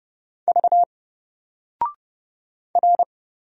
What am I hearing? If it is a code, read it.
Morse code, heard as a keyed tone, about 30 wpm, 700 Hz. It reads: V R
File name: Single-Letter-Number-30wpm